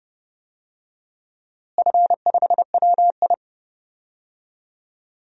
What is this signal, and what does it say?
Morse code, heard as a keyed tone, about 32 wpm, 700 Hz. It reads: F5WI
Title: DXCC-100 Call Signs - 4 Characters - Repeated 3x - 30wpm